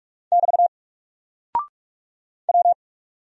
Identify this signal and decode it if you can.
Morse code, heard as a keyed tone, about 46 wpm, 700 Hz. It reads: <BT> W
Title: Single Character (Letter:Number:Punctuation:Prosign) 45wpm